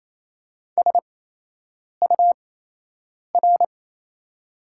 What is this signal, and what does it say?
Morse code, heard as a keyed tone, about 28 wpm, spaced as for 8 wpm, 700 Hz. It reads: SUR
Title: 8x Character Spacing - Top 500 Words - 28wpm